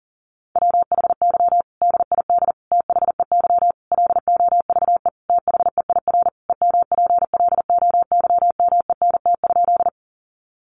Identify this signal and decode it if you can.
Morse code, heard as a keyed tone, about 40 wpm, 700 Hz. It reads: WHY DID THEY LOVE THEIR EMPLOYMENT?
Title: Sentences from Top 1500 Words 40wpm